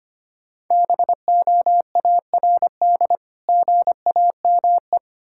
Morse code, heard as a keyed tone, about 25 wpm, 700 Hz. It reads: BOARD GAME
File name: practice / 25wpm / Difficult Words with Hints - 25wpm